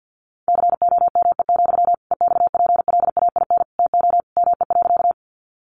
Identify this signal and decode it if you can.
Morse code, heard as a keyed tone, about 50 wpm, 700 Hz. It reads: COME, EXPLAIN TO ME.
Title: Sentences from Top 1100 Words 50wpm